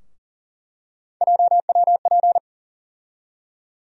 Morse code, heard as a keyed tone, about 40 wpm, 700 Hz. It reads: JWP
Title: USA Call Sign Postfixes - 3 Letters 40wpm